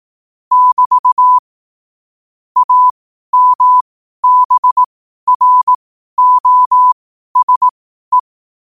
Morse code, heard as a keyed tone, about 18 wpm, spaced as for 12 wpm, 1000 Hz. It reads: <BT> AMBROSE
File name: qotdfiles / 12wpm